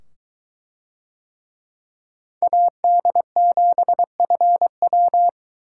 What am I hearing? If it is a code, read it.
Morse code, heard as a keyed tone, about 21 wpm, 700 Hz. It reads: AD7FW